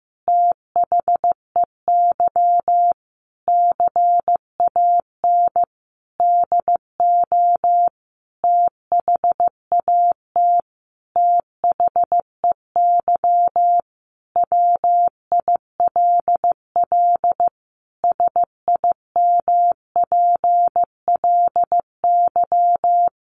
Morse code, heard as a keyed tone, about 15 wpm, 700 Hz. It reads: THEY CAN DO THAT THEY WILL SIMPLY
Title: Sentences from Top 600 Words 15wpm